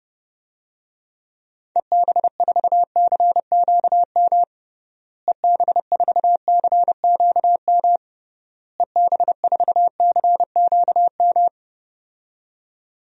Morse code, heard as a keyed tone, about 30 wpm, 700 Hz. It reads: EB4CQM EB4CQM EB4CQM
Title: DXCC-100 Call Signs - 6 Characters - Repeated 3x - 30wpm